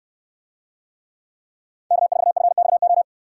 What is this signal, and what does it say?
Morse code, heard as a keyed tone, about 67 wpm, 700 Hz. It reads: K4FXO